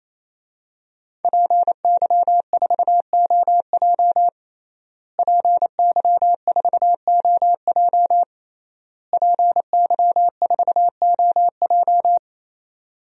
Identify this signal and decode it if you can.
Morse code, heard as a keyed tone, about 28 wpm, 700 Hz. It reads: PY4OJ PY4OJ PY4OJ
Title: DXCC-100 Call Signs - 5 Characters - Repeated 3x - 28wpm